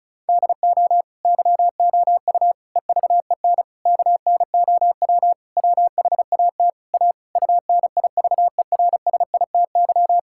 Morse code, heard as a keyed tone, about 36 wpm, 700 Hz. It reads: DO YOU EVEN KNOW WHAT A UNIVERSITY
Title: Sentences from Top 1000 Words - Review Part 5 - 35wpm